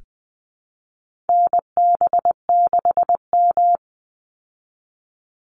Morse code, heard as a keyed tone, about 20 wpm, 700 Hz. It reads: NB6M
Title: Call Signs Repeated 2x - 20wpm